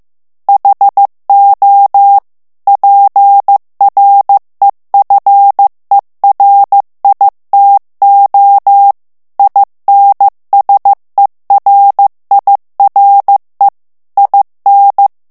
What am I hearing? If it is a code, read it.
Morse code, heard as a keyed tone, about 15 wpm, 800 Hz. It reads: HO PREFERITO INSERIRE IN
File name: ITALIEN texte clair 1 vitesse 14 mots minute